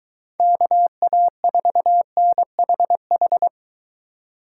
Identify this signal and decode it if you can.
Morse code, heard as a keyed tone, about 23 wpm, 700 Hz. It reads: KA4NHH